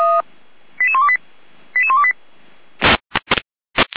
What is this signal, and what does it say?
Five-tone selective call (ZVEI), heard as quick runs of five tones, tones of about 80 ms. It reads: 88128 88128